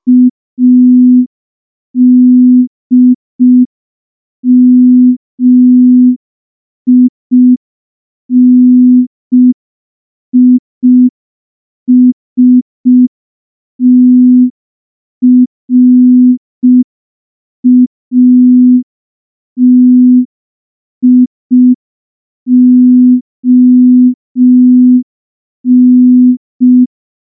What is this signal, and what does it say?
Morse code, heard as a keyed tone, about 5 wpm, 250 Hz. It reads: ADMINISTRATION
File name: morse / morse-training-05WPM